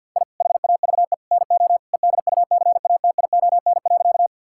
Morse code, heard as a keyed tone, about 48 wpm, 700 Hz. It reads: I HAVE NO EDUCATION.
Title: Sentences from Top 1000 Words - Review Part 6 - 50wpm